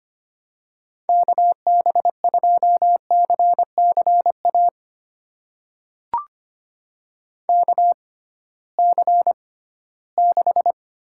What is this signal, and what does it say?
Morse code, heard as a keyed tone, about 25 wpm, spaced as for 16 wpm, 700 Hz. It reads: KB2CCA KC6
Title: USA Call Signs - 6 Characters - 6x Character Spacing - 25wpm